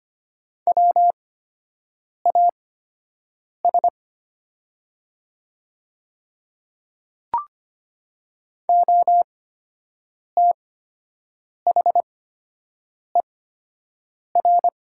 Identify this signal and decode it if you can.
Morse code, heard as a keyed tone, about 25 wpm, spaced as for 7 wpm, 700 Hz. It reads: WAS OTHER
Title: 8x Character Spacing - Top 100 Words - 25wpm